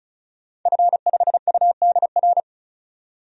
Morse code, heard as a keyed tone, about 34 wpm, 700 Hz. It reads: F5UDR
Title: DXCC-100 Call Signs - 5 Characters - 35wpm